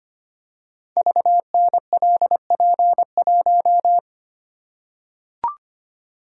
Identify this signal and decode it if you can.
Morse code, heard as a keyed tone, about 25 wpm, 700 Hz. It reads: VNLP1
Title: Code Groups - 5 Letter:Numbers - 25wpm